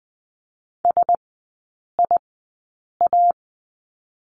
Morse code, heard as a keyed tone, about 20 wpm, spaced as for 8 wpm, 700 Hz. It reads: SIA
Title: 035 Lesson 7e - Letter s - Three Characters 20wpm